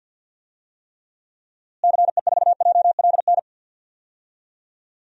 Morse code, heard as a keyed tone, about 51 wpm, 700 Hz. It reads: CE4JLN
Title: DXCC-100 Call Signs - 6 Characters - Repeated 3x - 50wpm